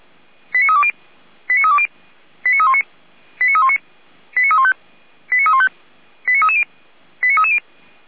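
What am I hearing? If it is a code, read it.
Five-tone selective call (ZVEI), heard as quick runs of five tones, tones of about 70 ms. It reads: 89320 89320 89319 89319 89315 89315 89330 89330
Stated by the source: Five-tone ZVEI selective call received with an SDR.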